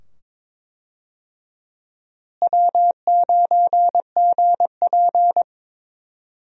Morse code, heard as a keyed tone, about 21 wpm, 700 Hz. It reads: W9GP
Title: USA Call Signs - 4 Characters - Repeated 2x - 22wpm